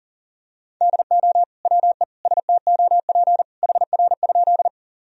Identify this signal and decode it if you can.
Morse code, heard as a keyed tone, about 37 wpm, 700 Hz. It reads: DO WE STOP HR?
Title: Sentences from Top 300 Words and Abbreviations 40wpm